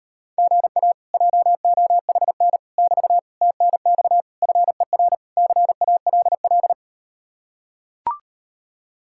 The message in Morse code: GA JOHN <BT> TNX FER CALL

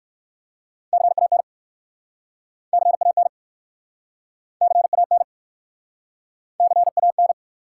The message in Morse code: CAN CAN CAN CAN

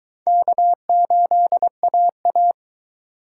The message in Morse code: K8AA

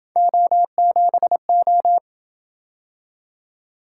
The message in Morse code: O7O

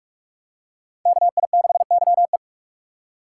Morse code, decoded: KI6YE

